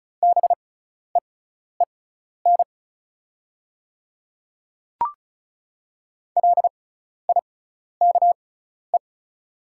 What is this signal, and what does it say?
Morse code, heard as a keyed tone, about 36 wpm, spaced as for 12 wpm, 700 Hz. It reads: BEEN LIKE
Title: 6x Character Spacing - Top 200 Words - 35wpm